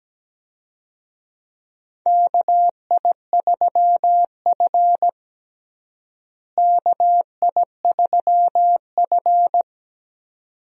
Morse code, heard as a keyed tone, about 17 wpm, 700 Hz. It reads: KI3F KI3F
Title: USA Call Signs - 4 Characters - Repeated 2x - 17wpm